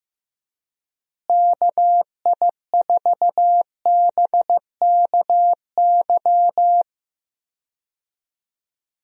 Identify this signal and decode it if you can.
Morse code, heard as a keyed tone, about 15 wpm, 700 Hz. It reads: KI4BKY
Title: USA Call Signs - 6 Characters - Repeated 2x - 15wpm